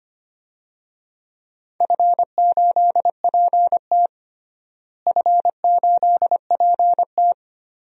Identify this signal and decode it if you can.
Morse code, heard as a keyed tone, about 25 wpm, 700 Hz. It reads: F8PT F8PT
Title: DXCC-100 Call Signs - 4 Characters - Repeated 3x - 25wpm